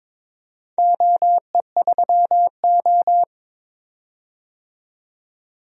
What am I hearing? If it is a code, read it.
Morse code, heard as a keyed tone, about 22 wpm, 700 Hz. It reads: OE3O
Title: DXCC-100 Call Signs - 4 Characters - 22wpm